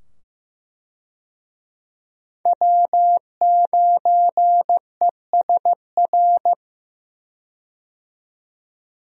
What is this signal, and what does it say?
Morse code, heard as a keyed tone, about 15 wpm, 700 Hz. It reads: W9ESR
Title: USA Call Signs - 5 Characters - 15wpm